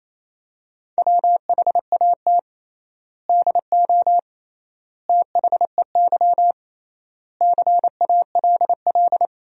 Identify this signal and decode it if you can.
Morse code, heard as a keyed tone, about 28 wpm, 700 Hz. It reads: WHAT DO THEY CALL
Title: 3x Word-Spacing - Sentences from Top 100 Words - 28wpm